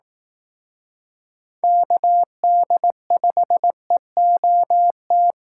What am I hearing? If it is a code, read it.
Morse code, heard as a keyed tone, about 18 wpm, 700 Hz. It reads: KD5EOT
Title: Speed Racing - USA 2x3 Call Signs 15wpm